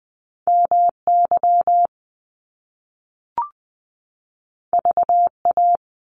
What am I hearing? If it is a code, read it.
Morse code, heard as a keyed tone, about 20 wpm, 700 Hz. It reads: MY VA
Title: Top Two Letter Combinations in Words 20wpm